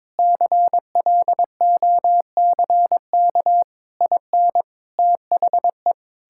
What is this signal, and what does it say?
Morse code, heard as a keyed tone, about 22 wpm, 700 Hz. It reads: CLOCK IN THE
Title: Pangram Challenge v2 - 22wpm